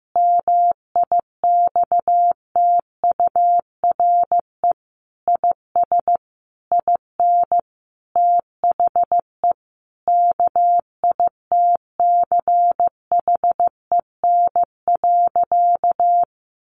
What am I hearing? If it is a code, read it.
Morse code, heard as a keyed tone, about 15 wpm, 700 Hz. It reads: MIXTURE IS IN THE KITCHEN.
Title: Sentences from Top 1500 Words 15wpm